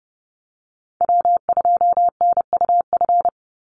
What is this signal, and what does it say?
Morse code, heard as a keyed tone, about 30 wpm, 700 Hz. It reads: W2NUF
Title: USA Call Signs 30wpm